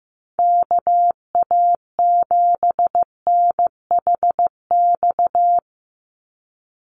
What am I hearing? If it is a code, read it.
Morse code, heard as a keyed tone, about 15 wpm, 700 Hz. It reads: KA7NHX
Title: USA Call Signs 15wpm